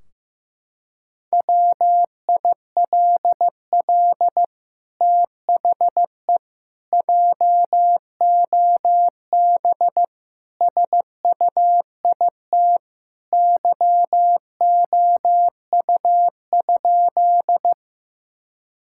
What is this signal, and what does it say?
Morse code, heard as a keyed tone, about 15 wpm, 700 Hz. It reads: WILL THE JOB SUIT YOU?
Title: Sentences from Top 1600 Words 15wpm